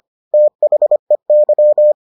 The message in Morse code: THEY

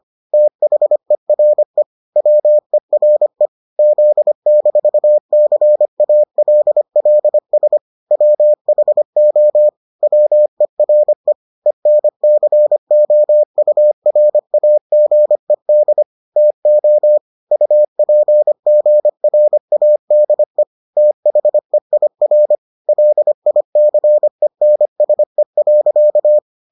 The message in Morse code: THERE WERE Z-CALLS WHO WERE ENCOURAGED TO UPGRADE THEIR LICENSE.

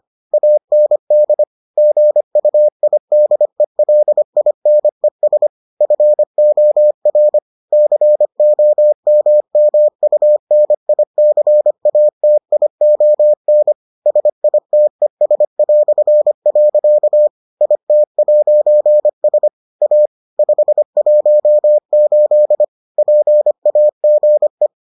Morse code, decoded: AND GUIDELINES FOR COMMUNICATION SITES". IT'S A 518 PAGE

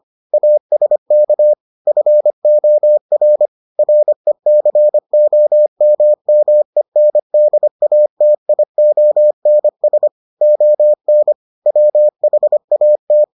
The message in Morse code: ASK FOR RECOMMENDATIONS ON WHAT

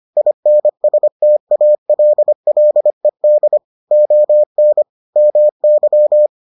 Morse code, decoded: INSTALLED ON MY

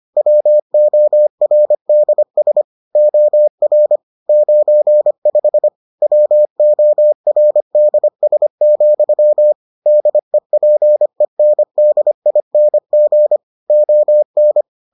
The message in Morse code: WORDS OR 95 WORDS, DEPENDING ON

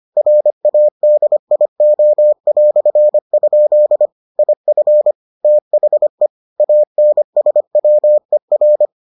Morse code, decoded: RADIO"? IF THE ANSWER

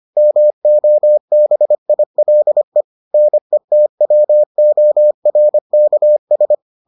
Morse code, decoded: MOBILE NETWORKS

